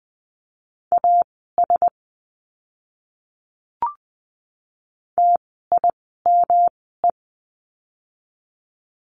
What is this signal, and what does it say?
Morse code, heard as a keyed tone, about 20 wpm, spaced as for 14 wpm, 700 Hz. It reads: AS TIME
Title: 115 Lesson 19f - Letter w - Words 20wpm